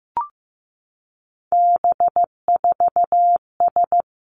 Morse code, heard as a keyed tone, about 15 wpm, 700 Hz. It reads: B4S
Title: Three Challenging Letter-Number Combinations 15wpm